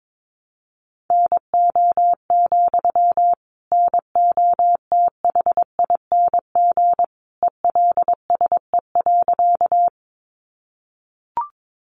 Morse code, heard as a keyed tone, about 22 wpm, 700 Hz. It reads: NO, NOTHING ELSE.
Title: Sentences from Top 800 Words 22wpm